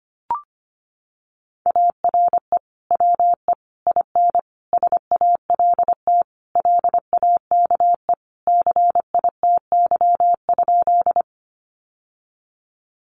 Morse code, heard as a keyed tone, about 25 wpm, 700 Hz. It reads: ARE WE IN SALT LAKE CITY?
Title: Sentences from Top 1100 Words 25wpm